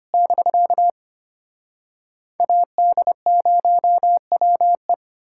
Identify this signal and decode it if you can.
Morse code, heard as a keyed tone, about 33 wpm, 700 Hz. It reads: <BK> AD0WE